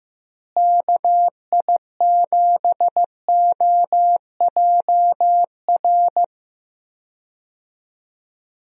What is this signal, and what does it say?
Morse code, heard as a keyed tone, about 15 wpm, 700 Hz. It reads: KI7OJR
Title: USA Call Signs - 6 Characters - Repeated 2x - 15wpm